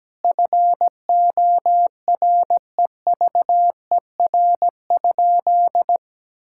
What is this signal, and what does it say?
Morse code, heard as a keyed tone, about 17 wpm, 700 Hz. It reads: FOREVER?
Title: Sentences from Top 1600 Words 17wpm